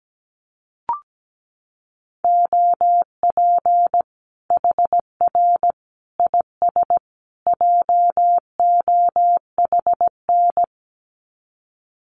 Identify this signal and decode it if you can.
Morse code, heard as a keyed tone, about 17 wpm, 700 Hz. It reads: OP HR IS JOHN
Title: Common QSO Phrases 17wpm